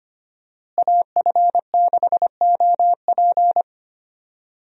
Morse code, heard as a keyed tone, about 25 wpm, 700 Hz. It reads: AF6OP
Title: USA Call Signs - 5 Characters - Repeated 2x - 25wpm